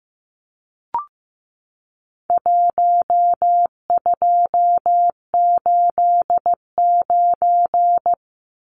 Morse code, heard as a keyed tone, about 15 wpm, 700 Hz. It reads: 1289
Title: Numbers - Contest Style - 4-Digit 15wpm